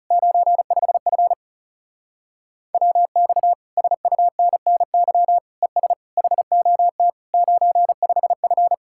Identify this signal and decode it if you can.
Morse code, heard as a keyed tone, about 36 wpm, 700 Hz. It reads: 95F WX SUNNY ES HOT 95F
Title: Speed-Racing Common QSO Phrases - 30wpm